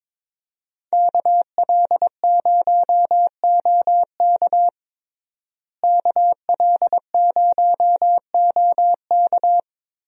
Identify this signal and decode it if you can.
Morse code, heard as a keyed tone, about 22 wpm, 700 Hz. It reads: KL0OK KL0OK
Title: USA Call Signs - 5 Characters - Repeated 2x - 22wpm